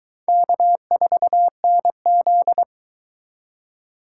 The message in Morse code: K4NZ

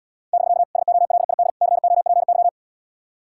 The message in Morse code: <BK> SOLID COPY

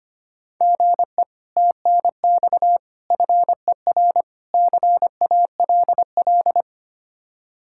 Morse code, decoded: GE TNX FER CALL

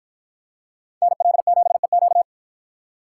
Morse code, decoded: NL7EQ

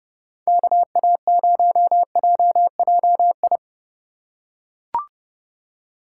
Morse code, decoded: KA0JJI

